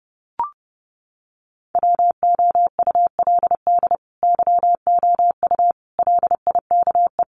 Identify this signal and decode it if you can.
Morse code, heard as a keyed tone, about 30 wpm, 700 Hz. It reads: WOULD YOU LIKE
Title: Sentences from Top 100 Words 30wpm